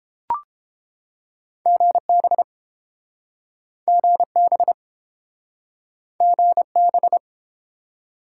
Morse code, heard as a keyed tone, about 30 wpm, 700 Hz. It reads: GB GB GB